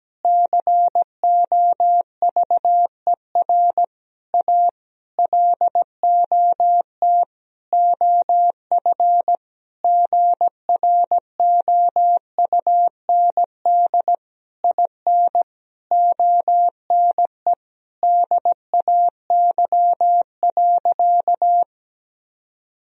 Morse code, heard as a keyed tone, about 17 wpm, 700 Hz. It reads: COVER A LOT OF GROUND IN ONE DAY.
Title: Sentences from Top 1000 Words - Review Part 3 - 17wpm